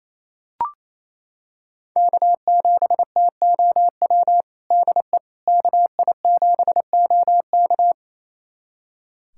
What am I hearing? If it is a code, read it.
Morse code, heard as a keyed tone, about 27 wpm, 700 Hz. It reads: K7TOW DE KI7OK
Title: USA Callsign DE Callsign - 5 Characters - 28wpm